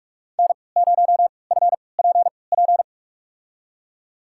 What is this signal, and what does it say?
Morse code, heard as a keyed tone, about 41 wpm, spaced as for 25 wpm, 700 Hz. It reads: N0FPP
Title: USA Call Signs - 3x Character Spacing - 45wpm